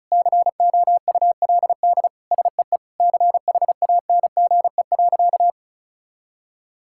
Morse code, heard as a keyed tone, about 36 wpm, 700 Hz. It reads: COULD SEE CHANGE.
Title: Sentences from Top 1000 Words - Review Part 5 - 35wpm